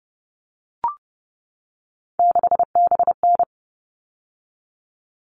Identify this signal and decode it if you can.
Morse code, heard as a keyed tone, about 30 wpm, 700 Hz. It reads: 6BN
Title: All Cut Numbers - 3-Digit 30wpm